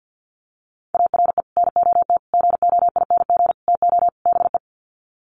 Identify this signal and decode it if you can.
Morse code, heard as a keyed tone, about 50 wpm, 700 Hz. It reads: ARE NOT GOING TO BE
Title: Sets of 5 Words Encore 50wpm